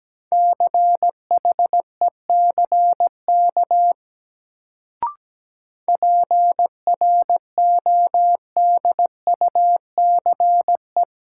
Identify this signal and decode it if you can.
Morse code, heard as a keyed tone, about 17 wpm, 700 Hz. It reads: CHECK PRODUCE
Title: Sentences from Top 1000 Words - Review Part 3 - New Word Drill - 17wpm